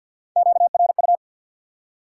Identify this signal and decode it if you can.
Morse code, heard as a keyed tone, about 53 wpm, 700 Hz. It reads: QRU
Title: Top 90 QSO Elements 50wpm